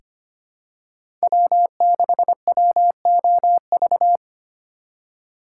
Morse code, heard as a keyed tone, about 25 wpm, 700 Hz. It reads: W6WOV
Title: USA Call Signs - 5 Characters - 4x Character Spacing - 25wpm